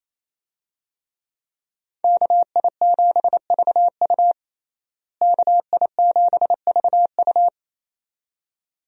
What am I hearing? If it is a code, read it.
Morse code, heard as a keyed tone, about 27 wpm, 700 Hz. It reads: KI7VU KI7VU
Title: USA Call Signs - 5 Characters - Repeated 2x - 28wpm